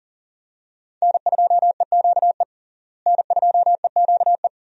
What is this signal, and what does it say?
Morse code, heard as a keyed tone, about 40 wpm, 700 Hz. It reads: N2EQE N2EQE
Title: USA Call Signs - 5 Characters - Repeated 2x - 40wpm